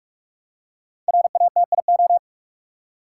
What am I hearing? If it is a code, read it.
Morse code, heard as a keyed tone, about 41 wpm, 700 Hz. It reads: RATIO
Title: Five Letter Words - 45wpm